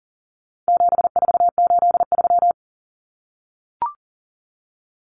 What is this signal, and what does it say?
Morse code, heard as a keyed tone, about 39 wpm, 700 Hz. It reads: 7483
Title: Common Cut Numbers - 4-Digit 40wpm